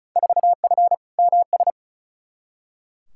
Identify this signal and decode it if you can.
Morse code, heard as a keyed tone, about 36 wpm, 700 Hz. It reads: 4F MS
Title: Field Day Exchanges Encore - 35wpm